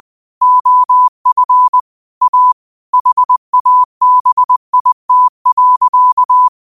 Morse code, heard as a keyed tone, about 20 wpm, 1000 Hz. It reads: OF A HABIT.